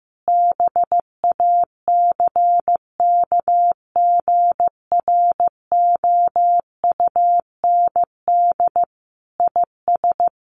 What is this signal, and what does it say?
Morse code, heard as a keyed tone, about 15 wpm, 700 Hz. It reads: BACKGROUND IS
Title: Sentences from Top 1300 Words 15wpm